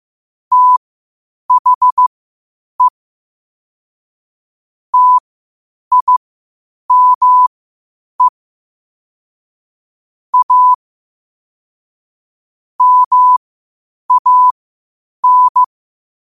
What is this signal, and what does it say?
Morse code, heard as a keyed tone, about 15 wpm, spaced as for 8 wpm, 1000 Hz. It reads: THE TIME A MAN
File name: qotdfiles / 8wpm